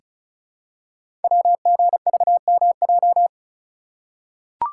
Morse code, heard as a keyed tone, about 35 wpm, 690 Hz. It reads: WGVMJ
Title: Code Groups - 5 Letters - 35wpm